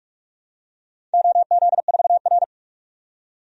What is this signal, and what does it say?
Morse code, heard as a keyed tone, about 44 wpm, 700 Hz. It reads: OZ4R
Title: DXCC-100 Call Signs - 4 Characters - 45wpm